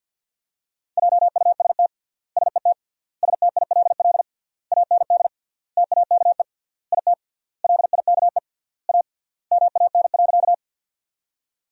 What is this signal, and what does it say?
Morse code, heard as a keyed tone, about 47 wpm, 700 Hz. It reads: JUST SET STILL AND TAKE IT LIKE A MAN.
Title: 3x Word-Spacing - Sentences from Top 300 Words - 50wpm